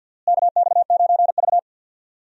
Morse code, heard as a keyed tone, about 45 wpm, 700 Hz. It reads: KX9V
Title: USA Call Signs - 4 Characters - Repeated 2x - 50wpm